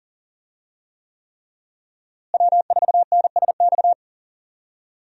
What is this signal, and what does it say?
Morse code, heard as a keyed tone, about 40 wpm, 700 Hz. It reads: W4NSX
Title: USA Call Signs - 5 Characters - Repeated 3x - 40wpm